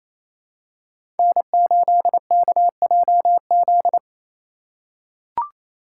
Morse code, heard as a keyed tone, about 27 wpm, 700 Hz. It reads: N8KJZ
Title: Code Groups - 5 Letter:Numbers - 28wpm